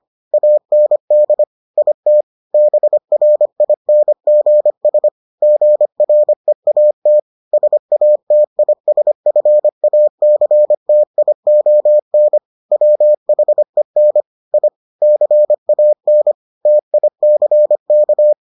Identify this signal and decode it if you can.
Morse code, heard as a keyed tone, about 25 wpm, 600 Hz. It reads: AND IT BRINGS GREAT SATISFACTION WHEN I CAN TICK